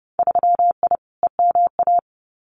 Morse code, heard as a keyed tone, about 31 wpm, 700 Hz. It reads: 3I EMA
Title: Winter Field Day Exchanges Encore 30wpm